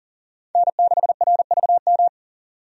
Morse code, heard as a keyed tone, about 40 wpm, 700 Hz. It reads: N6RVM
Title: USA Call Signs - 5 Characters - 40wpm